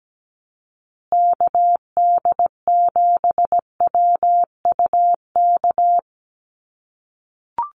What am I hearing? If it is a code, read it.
Morse code, heard as a keyed tone, about 17 wpm, 700 Hz. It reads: KD7WUK
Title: USA Call Signs Repeated 3x 17wpm